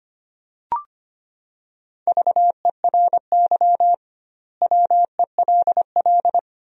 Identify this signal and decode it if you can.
Morse code, heard as a keyed tone, about 25 wpm, 700 Hz. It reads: VERY WELL
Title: 2x Word-Spacing - Sentences from Top 200 Words - 25wpm